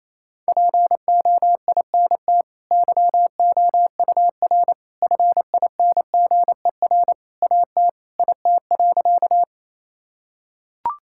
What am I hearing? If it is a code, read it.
Morse code, heard as a keyed tone, about 28 wpm, 700 Hz. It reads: POINT YOUR FINGER AT IT.
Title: Sentences from Top 1600 Words 28wpm